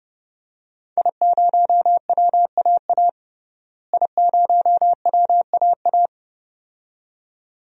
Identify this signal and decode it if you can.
Morse code, heard as a keyed tone, about 30 wpm, 700 Hz. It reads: I0WAA I0WAA
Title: DXCC-100 Call Signs - 5 Characters - Repeated 2x - 30wpm